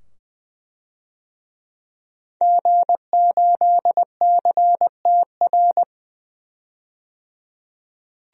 Morse code, heard as a keyed tone, about 20 wpm, 700 Hz. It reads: G8CTR
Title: DXCC-100 Call Signs - 5 Characters - 20wpm